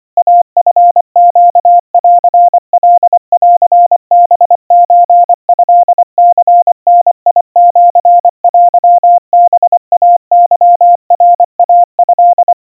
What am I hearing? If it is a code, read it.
Morse code (Japanese Wabun code), heard as a keyed tone, about 24 wpm, 700 Hz. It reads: イチネンカンハソトニダシテハイケナイト
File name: 90 cpm 700 Hz Crispy